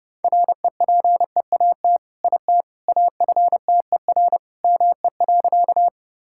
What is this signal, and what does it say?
Morse code, heard as a keyed tone, about 30 wpm, 700 Hz. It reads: REPEAT IT AFTER ME.
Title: Sentences from Top 1600 Words 30wpm